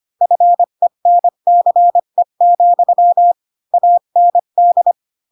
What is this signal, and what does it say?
Morse code, heard as a keyed tone, about 25 wpm, 700 Hz. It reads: FENCE, AND
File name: Pinocchio / Pinocchio ch2 20 25 ik1hge